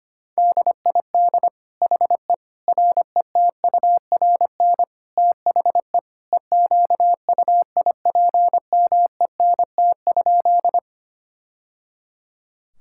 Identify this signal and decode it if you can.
Morse code, heard as a keyed tone, about 25 wpm, 700 Hz. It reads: DID HE RETURN THE EQUIPMENT?
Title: Sentences from Top 1000 Words - Review Part 5 - 25wpm